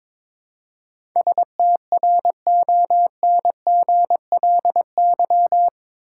Morse code, heard as a keyed tone, about 22 wpm, 700 Hz. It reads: STRONGLY
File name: practice / 22wpm / Eight Letter Words - 22wpm